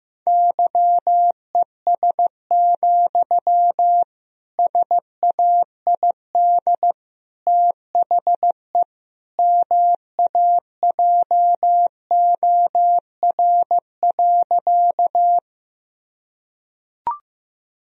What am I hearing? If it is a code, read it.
Morse code, heard as a keyed tone, about 15 wpm, 700 Hz. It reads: YES, SAID THE MAJOR.
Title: Sentences from Top 1000 Words - Review Part 4 - 15wpm